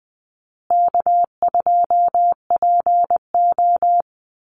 Morse code, heard as a keyed tone, about 20 wpm, 700 Hz. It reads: K2PO
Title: Easy Call Signs Repeated 3x - 20wpm